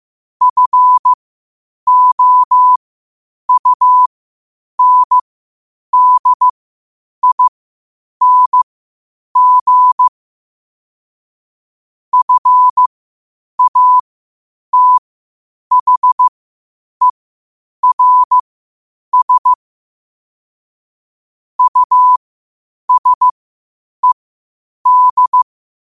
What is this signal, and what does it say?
Morse code, heard as a keyed tone, about 15 wpm, spaced as for 8 wpm, 1000 Hz. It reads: FOUNDING FATHERS USED
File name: qotdfiles / 8wpm